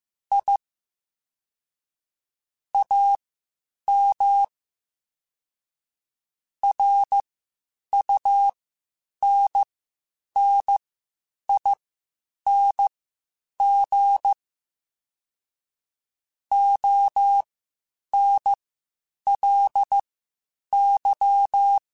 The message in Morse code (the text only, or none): I AM RUNNING ONLY